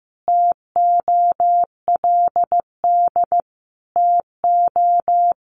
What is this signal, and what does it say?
Morse code, heard as a keyed tone, about 15 wpm, 700 Hz. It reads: TOLD TO
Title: Sentences from Top 1300 Words 15wpm